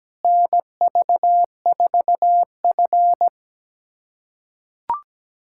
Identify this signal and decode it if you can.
Morse code, heard as a keyed tone, about 17 wpm, 700 Hz. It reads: NV4F
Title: USA Callsign DE Callsign - 4 Characters - 17wpm